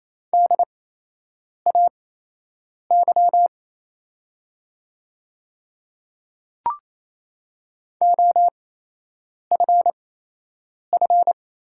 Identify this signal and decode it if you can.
Morse code, heard as a keyed tone, about 28 wpm, spaced as for 8 wpm, 700 Hz. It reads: DAY OFF
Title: 8x Character Spacing - Top 300 Words - 28wpm